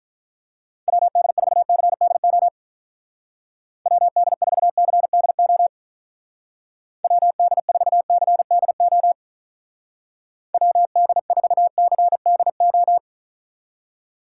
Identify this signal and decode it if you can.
Morse code, heard as a keyed tone, about 45 wpm, 700 Hz. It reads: WD4CDO WD4CDO WD4CDO WD4CDO